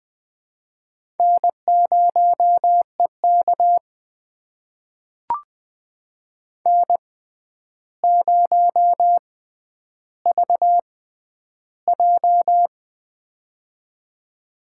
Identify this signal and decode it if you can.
Morse code, heard as a keyed tone, about 20 wpm, spaced as for 10 wpm, 700 Hz. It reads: N0EK N0VJ
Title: USA Call Signs - 4 Characters - 6x Character Spacing - 20wpm